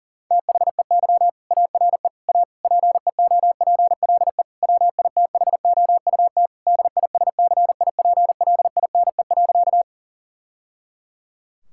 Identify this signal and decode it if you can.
Morse code, heard as a keyed tone, about 40 wpm, 700 Hz. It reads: THEY ARE A PEOPLE WITHOUT DISCIPLINE.
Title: Sentences from Top 1000 Words - Review Part 6 - 40wpm